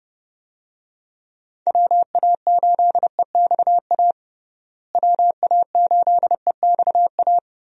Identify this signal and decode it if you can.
Morse code, heard as a keyed tone, about 31 wpm, 700 Hz. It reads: WA8EXA WA8EXA
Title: USA Call Signs - 6 Characters - Repeated 2x - 30wpm